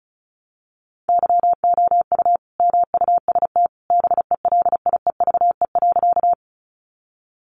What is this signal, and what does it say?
Morse code, heard as a keyed tone, about 34 wpm, 700 Hz. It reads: YOU MUST BELIEVE.
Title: Sentences from Top 1000 Words 35wpm